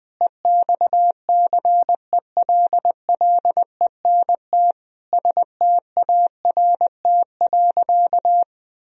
Morse code, heard as a keyed tone, about 20 wpm, 700 Hz. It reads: EXCELLENT START.
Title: Sentences from Top 1000 Words - Review Part 1 - 20wpm